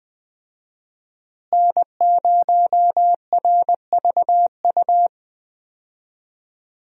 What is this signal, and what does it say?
Morse code, heard as a keyed tone, about 20 wpm, 700 Hz. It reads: N0RVU
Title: USA Call Signs - 5 Characters - Repeated 3x - 20wpm